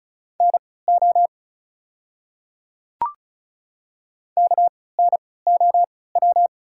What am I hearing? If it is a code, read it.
Morse code, heard as a keyed tone, about 36 wpm, spaced as for 20 wpm, 690 Hz. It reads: NO KNOW
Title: 3x Character Spacing - Top 100 Words - 35wpm